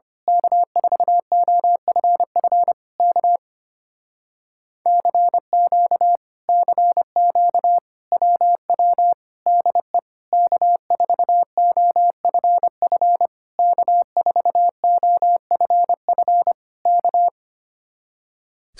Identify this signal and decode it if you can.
Morse code, heard as a keyed tone, about 24 wpm, 700 Hz. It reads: K4OFF K CQ CQ WW DE K4OFF K4OFF K
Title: Speed-Racing Common QSO Phrases Encore - 25wpm